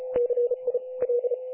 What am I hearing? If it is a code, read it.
Morse code, heard as a keyed tone, about 36 wpm, 500 Hz. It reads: CI L
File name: dlazaawansowanych / nagraniazpasma / websdr recording 3564.7kHz